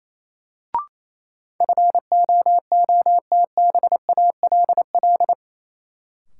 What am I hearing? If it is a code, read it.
Morse code, heard as a keyed tone, about 28 wpm, 700 Hz. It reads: FOOTBALL
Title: Sentences from Top 1600 Words - New Word Drill Alt - 28wpm